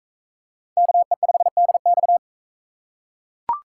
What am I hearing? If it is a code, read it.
Morse code, heard as a keyed tone, about 40 wpm, 700 Hz. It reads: KE5DX